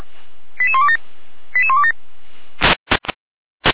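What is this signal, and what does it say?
Five-tone selective call (ZVEI), heard as quick runs of five tones, tones of about 70 ms. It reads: 88127 88127
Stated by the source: ZVEI five-tone call received by SDR.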